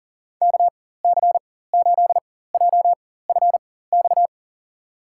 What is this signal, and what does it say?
Morse code, heard as a keyed tone, about 40 wpm, spaced as for 19 wpm, 700 Hz. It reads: KC8JFX
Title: USA Call Signs - 4x Character Spacing - 40wpm